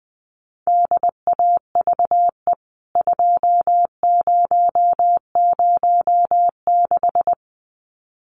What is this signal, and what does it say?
Morse code, heard as a keyed tone, about 20 wpm, 700 Hz. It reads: DAVE 2006
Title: CWT Exchange 20wpm